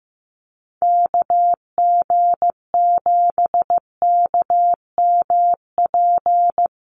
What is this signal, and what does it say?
Morse code, heard as a keyed tone, about 15 wpm, 700 Hz. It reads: KG7KMP
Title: USA Call Signs 15wpm